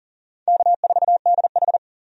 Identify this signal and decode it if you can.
Morse code, heard as a keyed tone, about 40 wpm, 700 Hz. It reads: K4DH
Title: USA Call Signs - 4 Characters - 40wpm